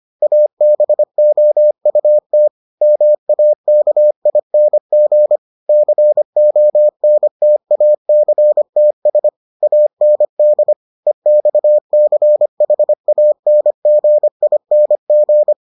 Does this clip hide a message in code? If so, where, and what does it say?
Yes: Morse code from 0.2 to 15.5 s, heard as a keyed tone, about 25 wpm, 600 Hz. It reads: ABOUT MAKING CONTACTS AND EXCHANGING